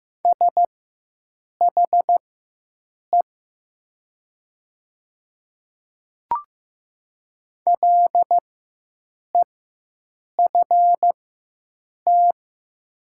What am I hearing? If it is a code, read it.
Morse code, heard as a keyed tone, about 15 wpm, spaced as for 7 wpm, 700 Hz. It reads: SHE LEFT
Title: 4x Character Spacing - Top 300 Words - 15wpm